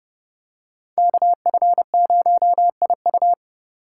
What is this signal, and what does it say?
Morse code, heard as a keyed tone, about 30 wpm, 700 Hz. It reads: KF0IU
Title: USA Call Signs - 4x Character Spacing - 30wpm